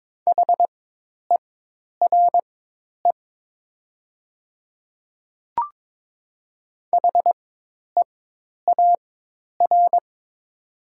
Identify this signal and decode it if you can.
Morse code, heard as a keyed tone, about 22 wpm, spaced as for 10 wpm, 700 Hz. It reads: HERE HEAR
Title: 4x Character Spacing - Top 300 Words - 22wpm